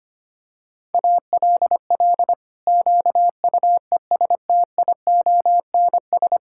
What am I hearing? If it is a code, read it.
Morse code, heard as a keyed tone, about 25 wpm, 700 Hz. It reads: ALL QUESTIONS
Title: Pangram Challenge v2 - 25wpm